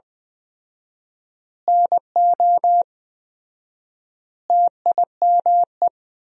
Morse code, heard as a keyed tone, about 20 wpm, 700 Hz. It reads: NO TIME
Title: 4x Word-Spacing - Sentences from Top 400 Words - 20wpm